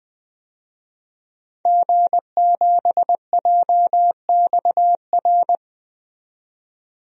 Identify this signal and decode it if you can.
Morse code, heard as a keyed tone, about 20 wpm, 700 Hz. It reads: G7JXR
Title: DXCC-100 Call Signs - 5 Characters - 20wpm